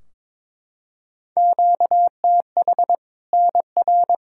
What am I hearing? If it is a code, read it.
Morse code, heard as a keyed tone, about 21 wpm, 700 Hz. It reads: QTH NR
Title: Practice QSO - Exchange 1 - Beginning - 22wpm